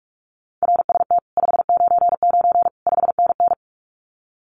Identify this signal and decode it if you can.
Morse code, heard as a keyed tone, about 46 wpm, 700 Hz. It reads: RST 599 5NN
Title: Common QSO Phrases Encore 45wpm